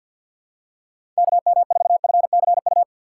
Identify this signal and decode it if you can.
Morse code, heard as a keyed tone, about 52 wpm, 700 Hz. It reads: KM4FCU
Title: USA Call Signs - 6 Characters - 6x Character Spacing - 50wpm